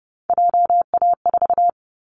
Morse code, heard as a keyed tone, about 30 wpm, 700 Hz. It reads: JA4
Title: Callsign Prefixes for Top 25 DXCC - Repeated 3x - 30wpm